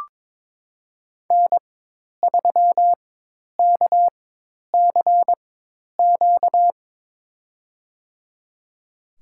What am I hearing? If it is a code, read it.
Morse code, heard as a keyed tone, about 22 wpm, spaced as for 10 wpm, 700 Hz. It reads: N3KCQ
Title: USA Call Signs - 5 Characters - 4x Character Spacing - 22wpm